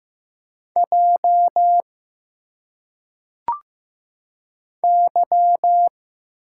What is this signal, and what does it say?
Morse code, heard as a keyed tone, about 15 wpm, 700 Hz. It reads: J Y
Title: Single Letters - Phonetic Pronunciation 15wpm